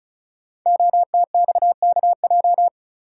Morse code, heard as a keyed tone, about 36 wpm, 700 Hz. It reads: OTXKJ
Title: Code Groups - 5 Letter:Numbers - 35wpm